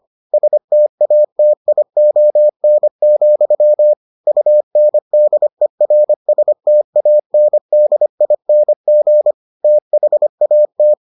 Morse code, heard as a keyed tone, about 25 wpm, 600 Hz. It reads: STATION, UNDERSTANDING THAT